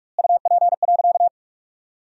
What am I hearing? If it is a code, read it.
Morse code, heard as a keyed tone, about 46 wpm, 700 Hz. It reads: UP.
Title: 2x Word-Spacing - Sentences from Top 500 Words - 45wpm